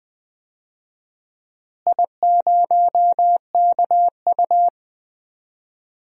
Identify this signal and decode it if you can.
Morse code, heard as a keyed tone, about 20 wpm, 700 Hz. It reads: I0KU
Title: DXCC-100 Call Signs - 4 Characters - Repeated 3x - 20wpm